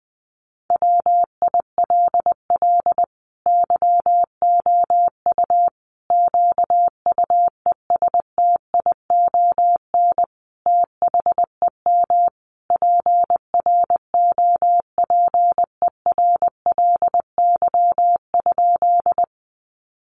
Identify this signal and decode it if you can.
Morse code, heard as a keyed tone, about 20 wpm, 700 Hz. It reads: WILL YOU QUESTION THEM PROPERLY?
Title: Sentences from Top 1100 Words 20wpm